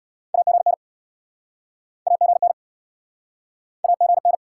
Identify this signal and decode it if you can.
Morse code, heard as a keyed tone, about 72 wpm, 700 Hz. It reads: AGN AGN AGN